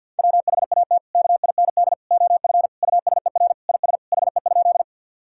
Morse code, heard as a keyed tone, about 50 wpm, 700 Hz. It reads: WHAT KIND OF USER IS HE?
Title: Sentences from Top 1000 Words - Review Part 5 - 50wpm